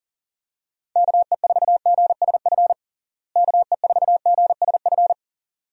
Morse code, heard as a keyed tone, about 40 wpm, 700 Hz. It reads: KE4GSF KE4GSF
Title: USA Call Signs - 6 Characters - Repeated 2x - 40wpm